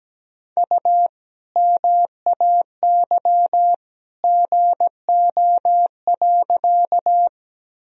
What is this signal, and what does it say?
Morse code, heard as a keyed tone, about 17 wpm, 700 Hz. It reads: U MAY GO.
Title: Sentences from Top 100 Words and Abbreviations 17wpm